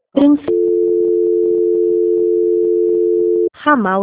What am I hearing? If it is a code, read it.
Dial tone (North America), 350 + 440 Hz, with speech.
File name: rg-1600-unknown- - - .70857